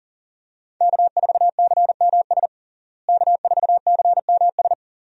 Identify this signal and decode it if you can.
Morse code, heard as a keyed tone, about 40 wpm, 700 Hz. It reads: K4CMS K4CMS
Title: USA Call Signs - 5 Characters - Repeated 2x - 40wpm